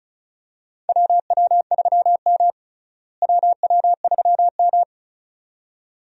Morse code, heard as a keyed tone, about 36 wpm, 700 Hz. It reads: WW3M WW3M
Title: USA Call Signs - 4 Characters - Repeated 2x - 35wpm